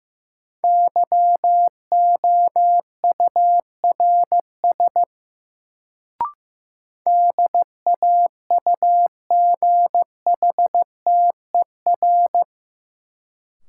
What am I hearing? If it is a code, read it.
Morse code, heard as a keyed tone, about 15 wpm, 700 Hz. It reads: YOURS DAUGHTER
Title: Sentences from Top 1500 Words - New Word Drill Alt - 15wpm